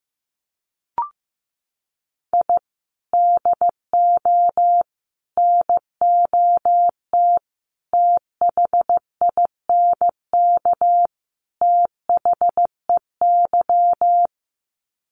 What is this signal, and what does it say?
Morse code, heard as a keyed tone, about 15 wpm, 700 Hz. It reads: I DO NOT THINK THEY
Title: Sets of 5 Words 15wpm